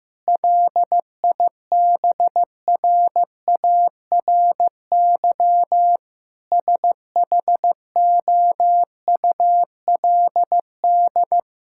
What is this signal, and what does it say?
Morse code, heard as a keyed tone, about 15 wpm, 700 Hz. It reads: LIBRARY SHOULD
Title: Sentences from Top 1000 Words - Review Part 5 - 15wpm